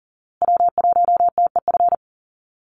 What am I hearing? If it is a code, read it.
Morse code, heard as a keyed tone, about 40 wpm, 700 Hz. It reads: W1TEF
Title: Call Signs Repeated 2x - 40wpm